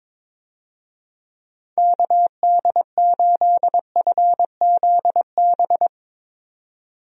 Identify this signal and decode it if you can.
Morse code, heard as a keyed tone, about 22 wpm, 700 Hz. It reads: KD8FZB
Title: USA Call Signs - 6 Characters - 22wpm